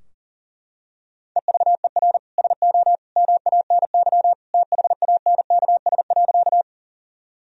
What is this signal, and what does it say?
Morse code, heard as a keyed tone, about 40 wpm, 700 Hz. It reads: EVER SO MANY THANKS.
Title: Sentences from Top 1000 Words - Review Part 3 - 40wpm